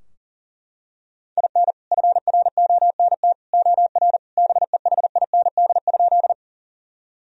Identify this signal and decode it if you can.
Morse code, heard as a keyed tone, about 40 wpm, 700 Hz. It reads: IN FRONT OR BEHIND?
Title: Sentences from Top 1000 Words - Review Part 2 - 40wpm